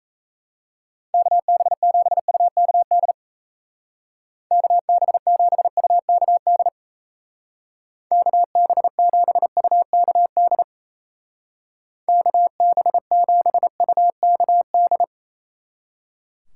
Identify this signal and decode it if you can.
Morse code, heard as a keyed tone, about 34 wpm, 700 Hz. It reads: KB7UKD KB7UKD KB7UKD KB7UKD
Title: Speed Racing - USA 2x3 Call Signs 28wpm